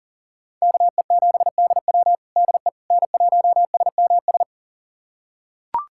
Morse code, heard as a keyed tone, about 37 wpm, 690 Hz. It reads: KE7DW DE N1SMS
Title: USA Callsign DE Callsign - 5 Characters - 40wpm